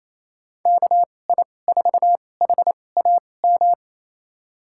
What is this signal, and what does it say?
Morse code, heard as a keyed tone, about 28 wpm, spaced as for 20 wpm, 700 Hz. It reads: KI4HAM
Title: USA Call Signs - 6 Characters - 2x Character Spacing - 28wpm